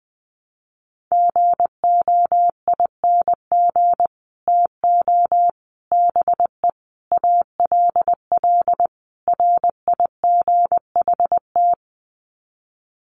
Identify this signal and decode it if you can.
Morse code, heard as a keyed tone, about 20 wpm, 700 Hz. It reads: GOING TO BE ALL RIGHT
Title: Sets of 5 Words Encore2 20wpm